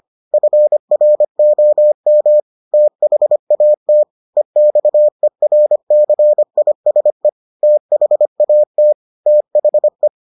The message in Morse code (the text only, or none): FROM THAT EXERCISE THAT THE